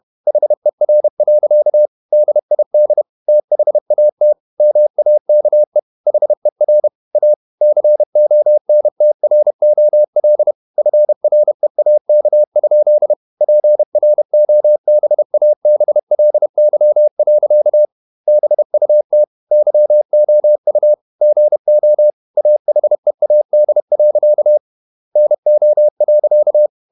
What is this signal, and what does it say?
Morse code, heard as a keyed tone, about 31 wpm, 600 Hz. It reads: HER. DID THAT MAKE HER A CONTROL FREAK? PROBABLY. BUT YOU GO AHEAD. NO.
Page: Never 31wpm | CW med Gnister